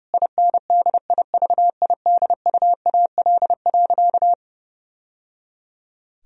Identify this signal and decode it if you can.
Morse code, heard as a keyed tone, about 30 wpm, 700 Hz. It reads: INDIVIDUAL.
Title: Sentences from Top 1000 Words - Review Part 4 - 30wpm